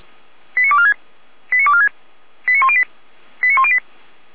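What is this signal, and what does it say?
Five-tone selective call (ZVEI), heard as quick runs of five tones, tones of about 70 ms. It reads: 89246 89246 89108 89108
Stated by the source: ZVEI five-tone selective call 89170 received on SDR.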